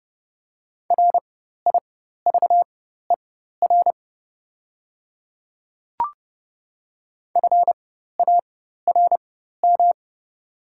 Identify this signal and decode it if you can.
Morse code, heard as a keyed tone, about 30 wpm, spaced as for 14 wpm, 700 Hz. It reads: RIVER FARM
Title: 4x Character Spacing - Top 500 Words - 30wpm